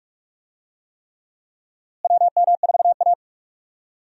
Morse code, heard as a keyed tone, about 46 wpm, 700 Hz. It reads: WM4A
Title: USA Call Signs - 4 Characters - Repeated 3x - 45wpm